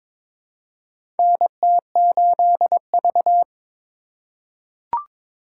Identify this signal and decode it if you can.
Morse code, heard as a keyed tone, about 22 wpm, 700 Hz. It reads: NT8V
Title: USA Call Signs - 3x Character Spacing - 22wpm